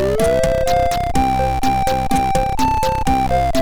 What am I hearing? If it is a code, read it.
There is no code.